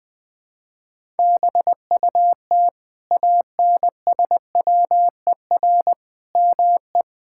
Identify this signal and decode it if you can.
Morse code, heard as a keyed tone, about 21 wpm, 700 Hz. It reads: BUT ANSWER ME
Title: Sentences from Top 1000 Words - Review Part 5 - 20wpm